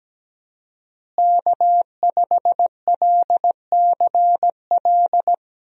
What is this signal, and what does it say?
Morse code, heard as a keyed tone, about 17 wpm, 700 Hz. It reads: K5LCL
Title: USA Call Signs - 5 Characters - Repeated 3x - 17wpm